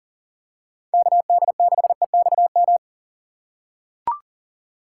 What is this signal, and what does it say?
Morse code, heard as a keyed tone, about 38 wpm, 700 Hz. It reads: KD6EXM
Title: USA Call Signs - 6 Characters - 3x Character Spacing - 40wpm